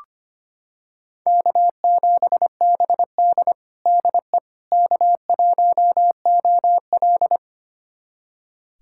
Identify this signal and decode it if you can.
Morse code, heard as a keyed tone, about 25 wpm, 700 Hz. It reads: K7BD DE K1OL
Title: USA Callsign DE Callsign - 4 Characters - 25wpm